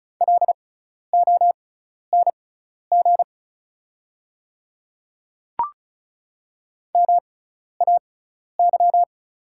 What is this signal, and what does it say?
Morse code, heard as a keyed tone, about 36 wpm, spaced as for 12 wpm, 700 Hz. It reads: LONG MAY
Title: 6x Character Spacing - Top 100 Words - 35wpm